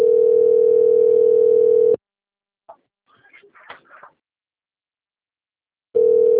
Ringback tone (North America), 440 + 480 Hz.